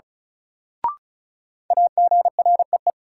Morse code, heard as a keyed tone, about 36 wpm, 700 Hz. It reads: AGREE
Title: Sets of 5 Words Encore - New Word Drill - 35wpm